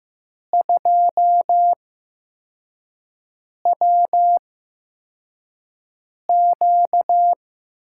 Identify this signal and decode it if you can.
Morse code, heard as a keyed tone, about 15 wpm, spaced as for 4 wpm, 700 Hz. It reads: 2WQ
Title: USA Call Signs - 8x Character Spacing - 15wpm